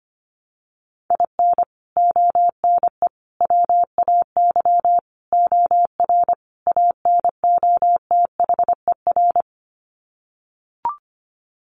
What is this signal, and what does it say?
Morse code, heard as a keyed tone, about 25 wpm, 690 Hz. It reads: IN ONE WAY OR ANOTHER
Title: Sets of 5 Words Encore 25wpm